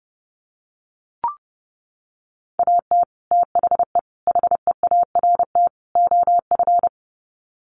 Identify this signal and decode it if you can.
Morse code, heard as a keyed tone, about 30 wpm, 700 Hz. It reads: AT THE HEART OF
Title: Sets of 4 Words 30wpm